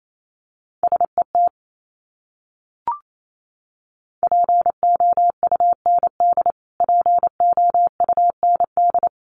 Morse code, heard as a keyed tone, about 28 wpm, 700 Hz. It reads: SET POUND POUND
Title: Top 500 Words Repeated 28wpm